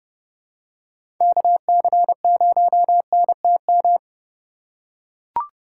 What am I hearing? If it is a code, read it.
Morse code, heard as a keyed tone, about 29 wpm, 700 Hz. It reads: KC0NTM